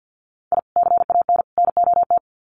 Morse code, heard as a keyed tone, about 50 wpm, 700 Hz. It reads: I CAN NOT
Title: Sets of 3 Words 50wpm